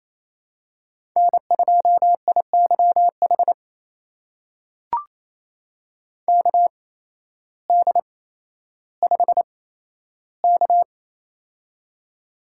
Morse code, heard as a keyed tone, about 28 wpm, spaced as for 13 wpm, 700 Hz. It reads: N2IYH KD5K
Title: USA Call Signs - 8x Character Spacing - 28wpm